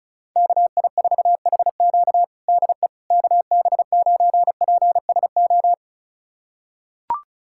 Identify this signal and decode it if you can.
Morse code, heard as a keyed tone, about 34 wpm, 700 Hz. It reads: KI4HQ DE KB9PSO
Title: USA Callsign DE Callsign 35wpm